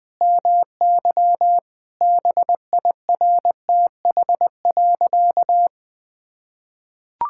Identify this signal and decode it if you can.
Morse code, heard as a keyed tone, about 20 wpm, 700 Hz. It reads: MY BIRTH.
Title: Sentences from Top 1000 Words - Review Part 5 - 20wpm